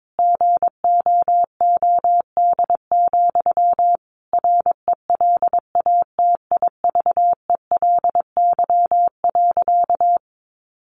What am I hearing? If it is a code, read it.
Morse code, heard as a keyed tone, about 22 wpm, 700 Hz. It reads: GOOD, RELATIVELY.
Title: Sentences from Top 1100 Words 22wpm